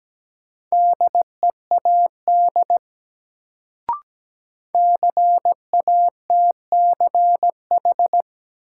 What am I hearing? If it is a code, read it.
Morse code, heard as a keyed tone, about 17 wpm, 700 Hz. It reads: DEAD CATCH
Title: Sentences from Top 1500 Words - New Word Drill - 17wpm